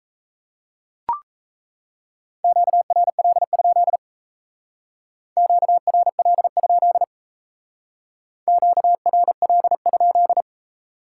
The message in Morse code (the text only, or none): QRL? QRL? QRL?